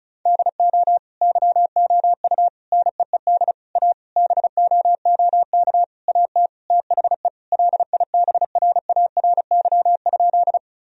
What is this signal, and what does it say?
Morse code, heard as a keyed tone, about 34 wpm, 700 Hz. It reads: DO YOU NEED A BOOK AT THE LIBRARY?
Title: Sentences from Top 1000 Words - Review Part 4 - 35wpm